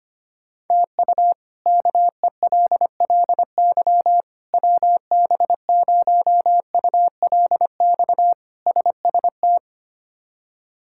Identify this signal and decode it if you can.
Morse code, heard as a keyed tone, about 25 wpm, 700 Hz. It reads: TU KELLY WB0ULX SST
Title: K1USN SST QSO Practice - 25wpm